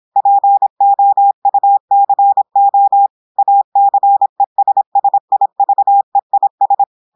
Morse code, heard as a keyed tone, about 26 wpm, 800 Hz. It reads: POUCO ACESSIVEIS
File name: audios / portugais / portugais texte en clair 1 vitesse 26 mots minutes